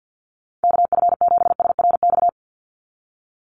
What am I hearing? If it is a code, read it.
Morse code, heard as a keyed tone, about 50 wpm, 700 Hz. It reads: KF7SRX
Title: USA Call Signs 50wpm